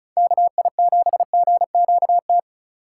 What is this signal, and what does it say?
Morse code, heard as a keyed tone, about 34 wpm, 700 Hz. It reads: KI7GQT
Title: USA Call Signs - 2x Character Spacing - 35wpm